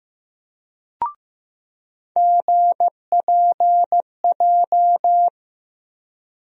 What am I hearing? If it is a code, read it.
Morse code, heard as a keyed tone, about 15 wpm, 700 Hz. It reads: GPJ
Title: USA Call Sign Postfixes - 3 Letters 15wpm